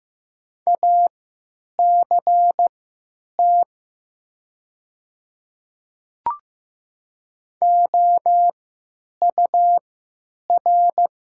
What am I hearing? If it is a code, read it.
Morse code, heard as a keyed tone, about 15 wpm, spaced as for 9 wpm, 700 Hz. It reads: ACT OUR
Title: 3x Character Spacing - Top 300 Words - 15wpm